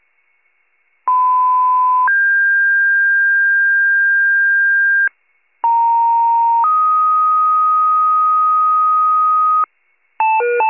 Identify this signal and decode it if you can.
Two-tone page, heard as a one-second tone, then a three-second tone, about 1010 then 1600 Hz, about 930 then 1250 Hz, with noise. It is heard from 1.1 to 9.6 s.
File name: wcalert